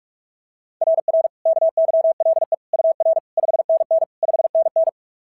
Morse code, heard as a keyed tone, about 44 wpm, 690 Hz. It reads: RR KYLE UR 5NN 5NN
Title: SOTA QSO Practice 45wpm